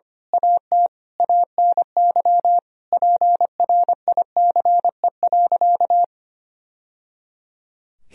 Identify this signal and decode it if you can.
Morse code, heard as a keyed tone, about 25 wpm, 700 Hz. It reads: AT ANY PRICE.
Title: Sentences from Top 1000 Words - Review Part 4 - 25wpm